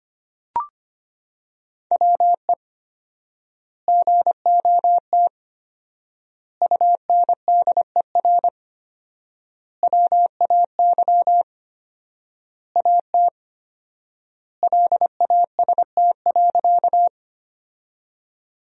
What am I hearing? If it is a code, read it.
Morse code, heard as a keyed tone, about 25 wpm, 700 Hz. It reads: WE GOT UNDER WAY AT LAST.
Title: 4x Word-Spacing - Sentences from Top 300 Words - 25wpm